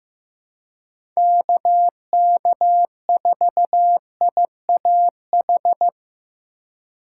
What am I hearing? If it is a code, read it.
Morse code, heard as a keyed tone, about 16 wpm, 700 Hz. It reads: KK4IAH
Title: USA Call Signs - 6 Characters - 2x Character Spacing - 15wpm